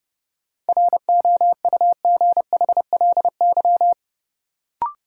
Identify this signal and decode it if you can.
Morse code, heard as a keyed tone, about 30 wpm, 700 Hz. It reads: ROUGHLY
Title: Sentences from Top 1500 Words - New Word Drill Alt - 30wpm